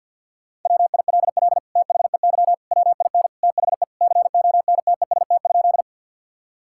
Morse code, heard as a keyed tone, about 50 wpm, 700 Hz. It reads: WILL THEY WIN THE CONTEST?
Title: Sentences from Top 1600 Words 50wpm